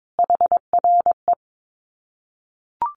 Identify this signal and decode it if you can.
Morse code, heard as a keyed tone, about 22 wpm, 700 Hz. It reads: HRE
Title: Common QSO Phrases 22wpm